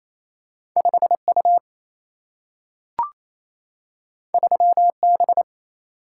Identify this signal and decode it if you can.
Morse code, heard as a keyed tone, about 28 wpm, 700 Hz. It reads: 5U 3B
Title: Single-Letter Single-Number 28wpm